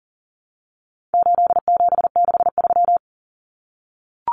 Morse code, heard as a keyed tone, about 40 wpm, 700 Hz. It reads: 8763
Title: Common Cut Numbers - 4-Digit 40wpm